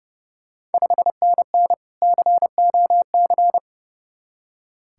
Morse code, heard as a keyed tone, about 30 wpm, 690 Hz. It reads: 5NN COC
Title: Ontario QSO Party Exchange 30wpm